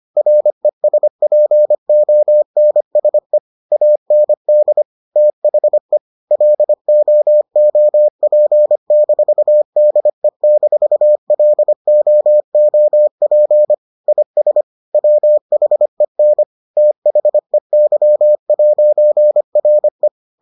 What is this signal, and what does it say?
Morse code, heard as a keyed tone, about 25 wpm, 600 Hz. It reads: RESPONSE AND THE LOOP-DE-LOOP IS WHEN THEY'RE